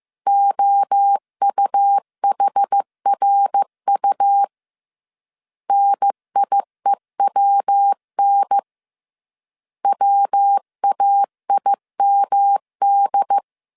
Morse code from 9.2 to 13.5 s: WAIMD